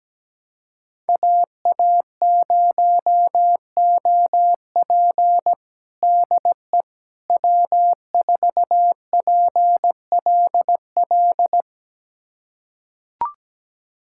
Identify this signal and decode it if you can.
Morse code, heard as a keyed tone, about 17 wpm, 700 Hz. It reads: AA0OP DE W4PLL
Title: USA Callsign DE Callsign - 5 Characters - 17wpm